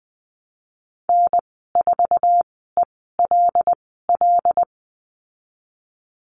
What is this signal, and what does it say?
Morse code, heard as a keyed tone, about 20 wpm, spaced as for 14 wpm, 700 Hz. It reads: N4ELL
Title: 128 Lesson 21g - Number 6 - Call Signs 20wpm